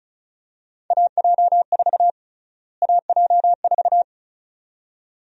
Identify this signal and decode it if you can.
Morse code, heard as a keyed tone, about 34 wpm, 700 Hz. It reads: AJ4 AJ4
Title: USA Call Sign Prefixes - Repeated 2x - 35wpm